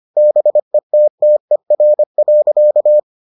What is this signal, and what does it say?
Morse code, heard as a keyed tone, about 25 wpm, 600 Hz. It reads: BETTER.